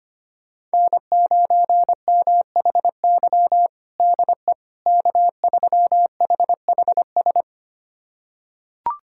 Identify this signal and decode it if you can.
Morse code, heard as a keyed tone, about 25 wpm, 700 Hz. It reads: N9MHY DE K3HHS
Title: USA Callsign DE Callsign - 5 Characters - 25wpm